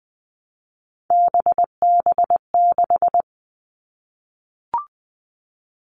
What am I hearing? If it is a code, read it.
Morse code, heard as a keyed tone, about 20 wpm, 700 Hz. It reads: BB6
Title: Three Challenging Letter-Number Combinations 20wpm